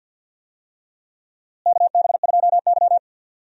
Morse code, heard as a keyed tone, about 52 wpm, 700 Hz. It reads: KB2Y
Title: USA Call Signs - 4 Characters - Repeated 3x - 50wpm